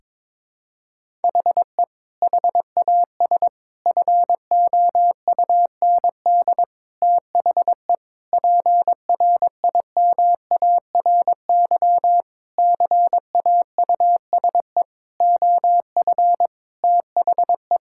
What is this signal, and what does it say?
Morse code, heard as a keyed tone, about 22 wpm, 700 Hz. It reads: HE HAS FOUND THE PRIMARY CAUSE OF THE
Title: Sentences from Top 1000 Words - Review Part 5 - 22wpm